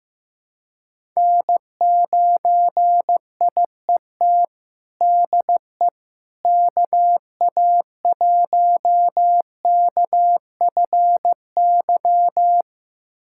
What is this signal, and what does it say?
Morse code, heard as a keyed tone, about 15 wpm, 700 Hz. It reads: N9IET DE KA1KFY
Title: USA Callsign DE Callsign 15wpm